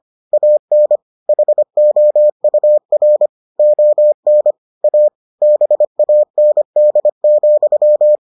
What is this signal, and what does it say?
Morse code, heard as a keyed tone, about 25 wpm, 600 Hz. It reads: AN HOUR ON A BAND,